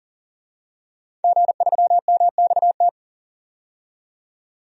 Morse code, heard as a keyed tone, about 40 wpm, 700 Hz. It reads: G3MXT
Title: DXCC-100 Call Signs - 5 Characters - 40wpm